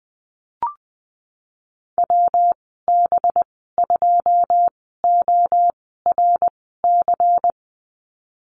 Morse code, heard as a keyed tone, about 20 wpm, spaced as for 15 wpm, 700 Hz. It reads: WB2ORC
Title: 186 Lesson 30g - Letter b - Call Signs 20wpm